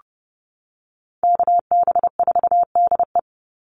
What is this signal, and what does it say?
Morse code, heard as a keyed tone, about 30 wpm, 700 Hz. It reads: KB4DE
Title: Call Signs - 30wpm